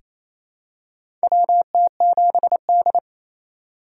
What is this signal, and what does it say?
Morse code, heard as a keyed tone, about 28 wpm, 700 Hz. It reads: WT7D
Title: USA Call Signs - 4 Characters - 4x Character Spacing - 28wpm